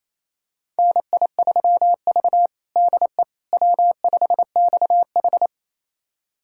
Morse code, heard as a keyed tone, about 28 wpm, 700 Hz. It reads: NI3V DE W5XH
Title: USA Callsign DE Callsign - 4 Characters - 28wpm